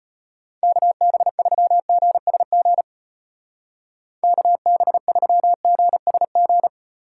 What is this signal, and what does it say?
Morse code, heard as a keyed tone, about 32 wpm, 700 Hz. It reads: KB3GSG KB3GSG